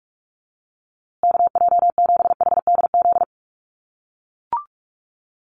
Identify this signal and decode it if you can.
Morse code, heard as a keyed tone, about 44 wpm, 700 Hz. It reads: KJ7HDZ
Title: USA Call Signs Repeated 3x 45wpm